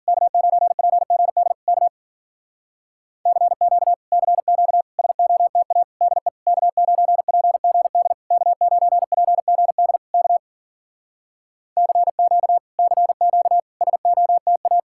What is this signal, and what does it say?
Morse code, heard as a keyed tone, about 52 wpm, 700 Hz. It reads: K9PGD K CQ CQ SOTA DE K9PGD K9PGD K CQ CQ SOTA